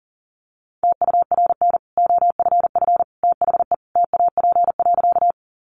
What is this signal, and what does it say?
Morse code, heard as a keyed tone, about 40 wpm, 700 Hz. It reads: TURN OFF THE TAP.
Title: Sentences from Top 1400 Words 40wpm